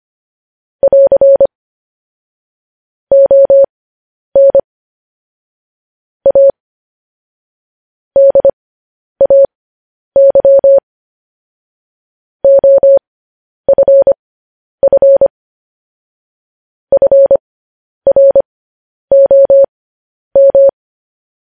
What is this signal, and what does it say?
Morse code, heard as a keyed tone, about 25 wpm, spaced as for 10 wpm, 550 Hz. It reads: <AR> ON A DAY OFF FROM
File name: podcast / News headlines-1x-25x10